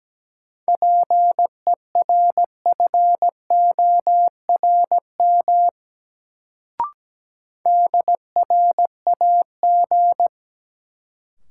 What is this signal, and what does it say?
Morse code, heard as a keyed tone, about 17 wpm, 700 Hz. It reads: PERFORM DRAG
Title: Sentences from Top 1500 Words - New Word Drill Alt - 17wpm